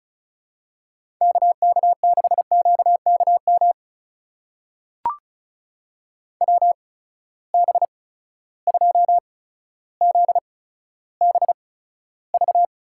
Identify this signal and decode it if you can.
Morse code, heard as a keyed tone, about 36 wpm, spaced as for 15 wpm, 700 Hz. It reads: KK6QKM WB2ZBV
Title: USA Call Signs - 8x Character Spacing - 35wpm